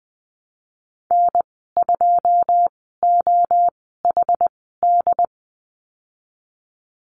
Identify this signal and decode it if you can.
Morse code, heard as a keyed tone, about 20 wpm, spaced as for 14 wpm, 700 Hz. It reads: N2OHD
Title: 128 Lesson 21g - Number 6 - Call Signs 20wpm